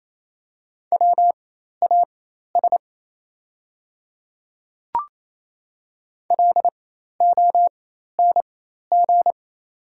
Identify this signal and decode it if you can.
Morse code, heard as a keyed tone, about 28 wpm, spaced as for 13 wpm, 700 Hz. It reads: WAS LONG
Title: 4x Character Spacing - Top 200 Words - 28wpm